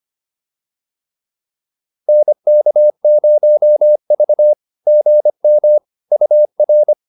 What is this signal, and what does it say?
Morse code, heard as a keyed tone, about 25 wpm, 600 Hz. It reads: NK0V GM UR